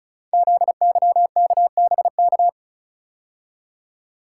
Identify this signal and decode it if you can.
Morse code, heard as a keyed tone, about 36 wpm, 690 Hz. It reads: ZYKBK
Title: Code Groups - 5 Letters - 35wpm